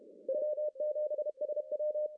Morse code, heard as a keyed tone, about 31 wpm, 600 Hz. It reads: W7SW
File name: CW / cqww-cw-k5tr